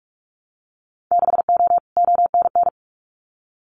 Morse code, heard as a keyed tone, about 44 wpm, 690 Hz. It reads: 6O ONN
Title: Winter Field Day Exchanges Encore 45wpm